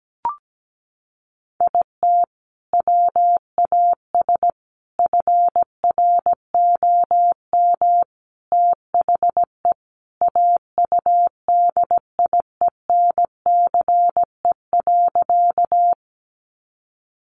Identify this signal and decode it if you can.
Morse code, heard as a keyed tone, about 17 wpm, 700 Hz. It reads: IT WAS FROM THE AUDIENCE.
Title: Sentences from Top 800 Words 17wpm